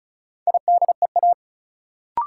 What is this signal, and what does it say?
Morse code, heard as a keyed tone, about 35 wpm, 700 Hz. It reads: IDEA
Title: Sentences from Top 300 Words - New Word Drill - 35wpm